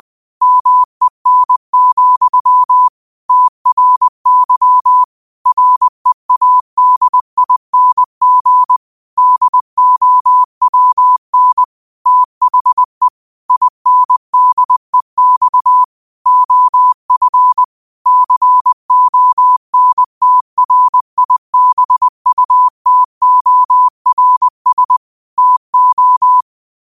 Morse code, heard as a keyed tone, about 20 wpm, 1000 Hz. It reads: MEN, TRY READING DOWN THE INDEX OF CONTRIBUTORS TO